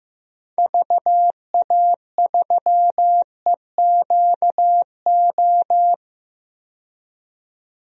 Morse code, heard as a keyed tone, about 15 wpm, 700 Hz. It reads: VA3EQO
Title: DXCC-100 Call Signs - 6 Characters - 15wpm